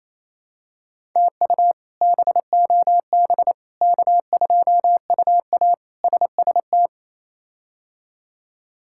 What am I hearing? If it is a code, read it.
Morse code, heard as a keyed tone, about 28 wpm, 700 Hz. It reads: TU BOB K2UA SST
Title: K1USN SST QSO Practice - 28wpm